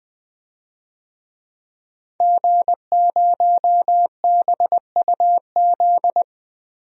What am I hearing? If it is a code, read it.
Morse code, heard as a keyed tone, about 20 wpm, 700 Hz. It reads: G0BUZ
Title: DXCC-100 Call Signs - 5 Characters - 20wpm